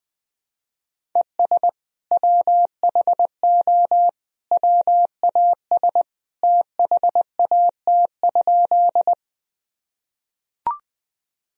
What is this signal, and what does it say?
Morse code, heard as a keyed tone, about 20 wpm, 700 Hz. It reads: ES WHO WAS THAT?
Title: Sentences from Top 100 Words and Abbreviations 20wpm